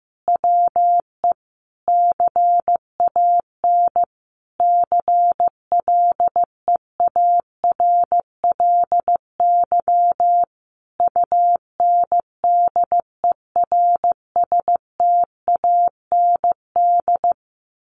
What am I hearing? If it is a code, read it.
Morse code, heard as a keyed tone, about 15 wpm, 700 Hz. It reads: WE CAN CLEARLY UNDERSTAND
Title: Sentences from Top 1400 Words 15wpm